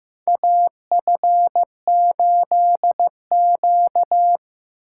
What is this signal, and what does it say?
Morse code, heard as a keyed tone, about 15 wpm, 700 Hz. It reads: AF8Q
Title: USA Call Signs - 2x Character Spacing - 15wpm